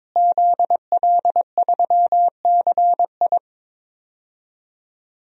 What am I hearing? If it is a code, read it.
Morse code, heard as a keyed tone, about 22 wpm, 700 Hz. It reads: ZL3CI
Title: DXCC-100 Call Signs - 5 Characters - Repeated 2x - 22wpm